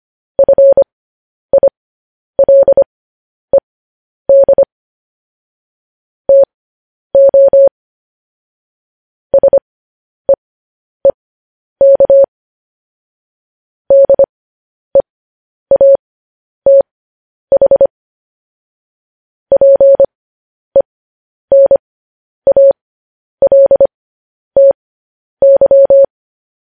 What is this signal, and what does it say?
Morse code, heard as a keyed tone, about 25 wpm, spaced as for 10 wpm, 550 Hz. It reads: FILED TO SEEK DEATH PENALTY
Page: news headlines in morse code at 10 wpm